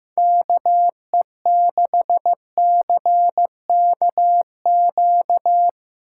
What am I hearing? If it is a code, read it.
Morse code, heard as a keyed tone, about 15 wpm, 700 Hz. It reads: KE6CKQ
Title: USA Call Signs - 6 Characters - 2x Character Spacing - 15wpm